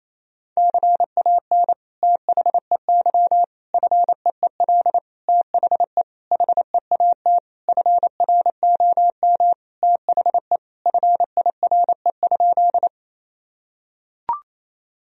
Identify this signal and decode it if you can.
Morse code, heard as a keyed tone, about 28 wpm, 700 Hz. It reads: CAN THEY FEEL THE HEAT FROM THE FIRE?
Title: Sentences from Top 1000 Words - Review Part 3 - 28wpm